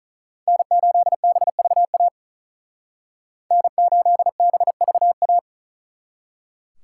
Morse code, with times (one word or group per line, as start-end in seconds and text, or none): 0.5-2.1 N8BVA
3.5-5.4 N8BVA